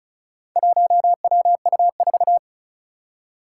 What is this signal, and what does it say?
Morse code, heard as a keyed tone, about 36 wpm, 700 Hz. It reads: 1WU4
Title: Four Challenging Letter-Number Combinations 35wpm